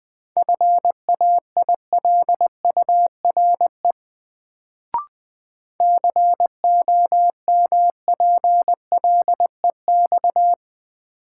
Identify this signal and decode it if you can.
Morse code, heard as a keyed tone, about 20 wpm, 700 Hz. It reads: FAILURE COMPLEX
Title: Sentences from Top 900 Words - New Word Drill - 20wpm